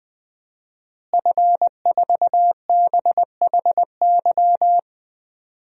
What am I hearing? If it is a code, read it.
Morse code, heard as a keyed tone, about 20 wpm, 700 Hz. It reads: F4BHY
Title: DXCC-100 Call Signs - 5 Characters - Repeated 3x - 20wpm